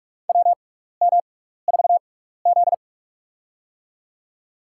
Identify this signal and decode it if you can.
Morse code, heard as a keyed tone, about 44 wpm, spaced as for 15 wpm, 700 Hz. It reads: WM4Z
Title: USA Call Signs - 4 Characters - 6x Character Spacing - 45wpm